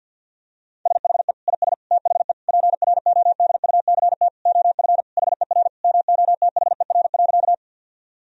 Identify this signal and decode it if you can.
Morse code, heard as a keyed tone, about 50 wpm, 700 Hz. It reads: SHE IS THE PRODUCT OF HER MOTHER.
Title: Sentences from Top 1000 Words - Review Part 2 - 50wpm